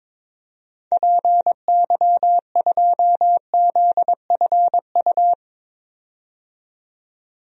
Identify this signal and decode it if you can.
Morse code, heard as a keyed tone, about 22 wpm, 700 Hz. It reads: PY2ZFU
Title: DXCC-100 Call Signs - 6 Characters - 22wpm